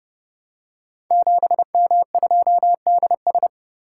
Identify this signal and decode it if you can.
Morse code, heard as a keyed tone, about 30 wpm, 700 Hz. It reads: 7M2DS
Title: Code Groups - 5 Letter:Numbers - 30wpm